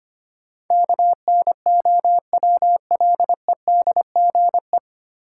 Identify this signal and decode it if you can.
Morse code, heard as a keyed tone, about 25 wpm, 700 Hz. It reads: KNOWLEDGE
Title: Sentences from Top 600 Words - New Word Drill - 25wpm